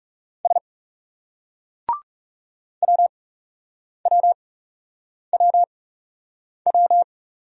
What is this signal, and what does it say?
Morse code, heard as a keyed tone, about 37 wpm, spaced as for 8 wpm, 700 Hz. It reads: S WWWW
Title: Speed-Racing Single Letter-Number 30wpm